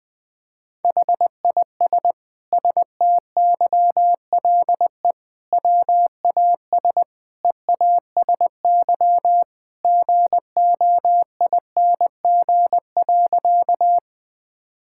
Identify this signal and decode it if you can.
Morse code, heard as a keyed tone, about 20 wpm, 700 Hz. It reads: HIS STYLE WAS EASY GOING.
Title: Sentences from Top 1000 Words - Review Part 5 - 20wpm